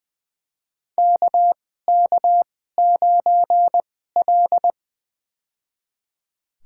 Morse code, heard as a keyed tone, about 20 wpm, spaced as for 15 wpm, 700 Hz. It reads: KK9L
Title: USA Call Signs - 4 Characters - 2x Character Spacing - 20wpm